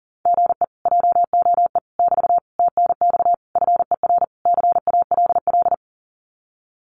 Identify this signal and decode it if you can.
Morse code, heard as a keyed tone, about 40 wpm, 700 Hz. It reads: GE JOE <BT> TNX FER CALL
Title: Common QSO Phrases Encore 40wpm